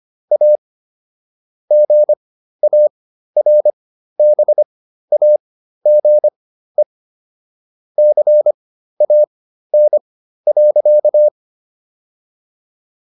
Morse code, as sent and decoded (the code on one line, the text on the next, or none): .- / --. .- .-. -... .- --. / -.-. .- -. .-.-.-
A GARBAGE CAN.